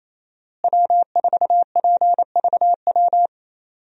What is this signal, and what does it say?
Morse code, heard as a keyed tone, about 28 wpm, 700 Hz. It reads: W4PVW
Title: USA Call Signs - 5 Characters - Repeated 3x - 28wpm